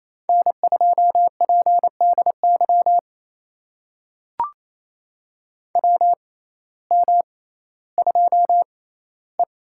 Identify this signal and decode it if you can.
Morse code, heard as a keyed tone, about 28 wpm, spaced as for 15 wpm, 700 Hz. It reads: N2PDY WM2E
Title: USA Call Signs - 6x Character Spacing - 28wpm